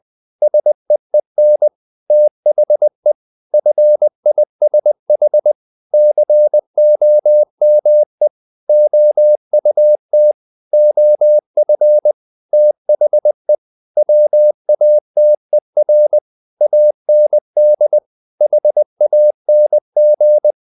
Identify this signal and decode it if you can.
Morse code, heard as a keyed tone, about 20 wpm, 600 Hz. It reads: SEEN THE FISH COME OUT OF THE WATER AND HANG
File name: the old man and the sea part 2 20wpm